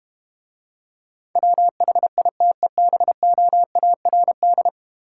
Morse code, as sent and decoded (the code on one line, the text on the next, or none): .-- .... .. - . -... --- .- .-. -..
WHITEBOARD